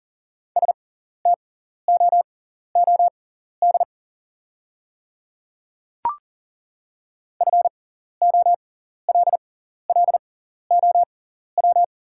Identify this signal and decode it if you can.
Morse code, heard as a keyed tone, about 40 wpm, spaced as for 14 wpm, 690 Hz. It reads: STOOD FOLLOW
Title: 6x Character Spacing - Top 500 Words - 40wpm